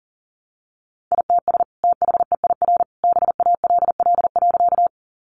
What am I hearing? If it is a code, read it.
Morse code, heard as a keyed tone, about 40 wpm, 700 Hz. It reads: ITS THEIR BALL.
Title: Sentences from Top 900 Words 40wpm